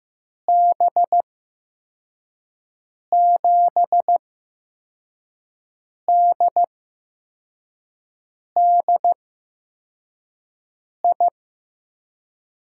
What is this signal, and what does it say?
Morse code, heard as a keyed tone, about 15 wpm, spaced as for 4 wpm, 700 Hz. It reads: B7DDI
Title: USA Call Signs - 6 Characters - 8x Character Spacing - 15wpm